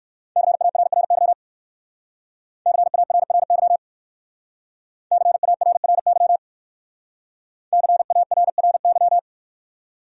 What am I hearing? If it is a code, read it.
Morse code, heard as a keyed tone, about 55 wpm, 700 Hz. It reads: CARRY CARRY CARRY CARRY
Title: Speed-Racing Top 300 Words 45wpm